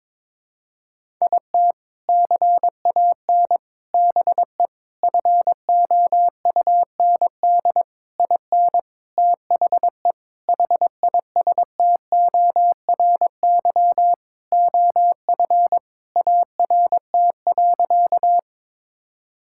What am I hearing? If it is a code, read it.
Morse code, heard as a keyed tone, about 22 wpm, 700 Hz. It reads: IT CAN BE FOUND IN THE HISTORY OF ART.
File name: practice / 22wpm / Sentences with 4-Word Phrases - 22wpm